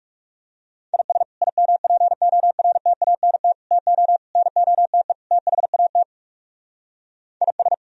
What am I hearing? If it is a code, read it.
Morse code, heard as a keyed tone, about 39 wpm, 700 Hz. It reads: IS IMPORTANT TO NOTE THAT IS